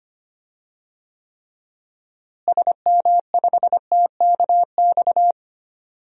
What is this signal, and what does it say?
Morse code, heard as a keyed tone, about 25 wpm, 700 Hz. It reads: SM5TKX
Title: DXCC-100 Call Signs - 6 Characters - Repeated 2x - 25wpm